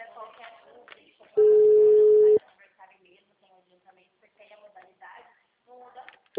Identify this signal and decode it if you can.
Ringback tone (Europe, 425 Hz), with speech.
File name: out- -2014- - - .58817